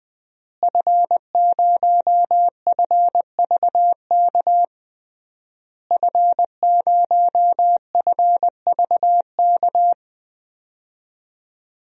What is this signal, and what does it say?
Morse code, heard as a keyed tone, about 20 wpm, 700 Hz. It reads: F0FVK F0FVK
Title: DXCC-100 Call Signs - 5 Characters - Repeated 3x - 20wpm